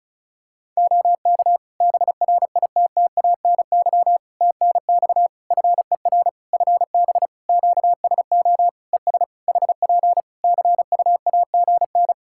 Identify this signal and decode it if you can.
Morse code, heard as a keyed tone, about 34 wpm, 700 Hz. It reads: OK BRITTANY TNX FER FB QSO ES HP CUAGN
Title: Practice QSO - Exchange 4 - Ending - 35wpm